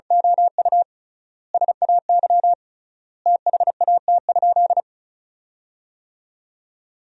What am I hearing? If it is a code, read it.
Morse code, heard as a keyed tone, about 34 wpm, 700 Hz. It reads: OU SAY THAT?
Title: 3x Word-Spacing - Sentences from Top 200 Words - 35wpm